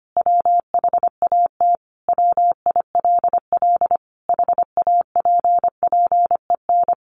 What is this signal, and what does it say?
Morse code, heard as a keyed tone, about 25 wpm, 700 Hz. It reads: WHAT WILL HAPPEN
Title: Sentences from Top 500 Words 25wpm